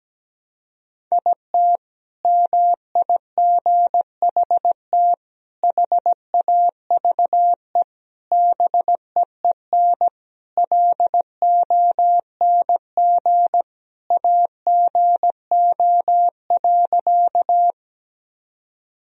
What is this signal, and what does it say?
Morse code, heard as a keyed tone, about 17 wpm, 700 Hz. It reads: IT MIGHT HAVE BEEN LONG AGO.
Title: Sentences from Top 1000 Words - Review Part 3 - 17wpm